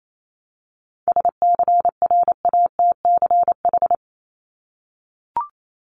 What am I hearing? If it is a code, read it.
Morse code, heard as a keyed tone, about 28 wpm, 700 Hz. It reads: SCRATCH
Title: Top 1501-1750 Words 28wpm